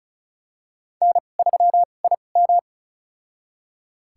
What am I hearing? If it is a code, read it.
Morse code, heard as a keyed tone, about 34 wpm, spaced as for 25 wpm, 700 Hz. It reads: N3IM
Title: USA Call Signs - 4 Characters - 2x Character Spacing - 35wpm